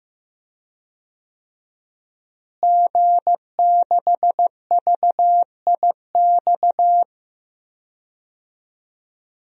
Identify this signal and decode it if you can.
Morse code, heard as a keyed tone, about 15 wpm, 700 Hz. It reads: G6VIX
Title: DXCC-100 Call Signs - 5 Characters - Repeated 2x - 15wpm